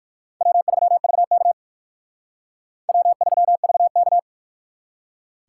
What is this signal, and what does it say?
Morse code, heard as a keyed tone, about 55 wpm, 700 Hz. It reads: W3VK W3VK
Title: Speed Racing - USA 1x2 Call Signs 45wpm